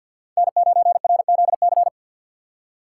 Morse code, heard as a keyed tone, about 47 wpm, 700 Hz. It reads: N9RZC
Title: USA Call Signs - 5 Characters - 4x Character Spacing - 50wpm